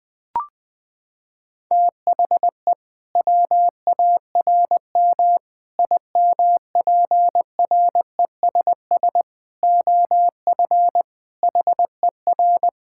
Morse code, heard as a keyed tone, about 20 wpm, 700 Hz. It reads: THE WARM IMPRESS OF HER
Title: Sentences from Top 1600 Words 20wpm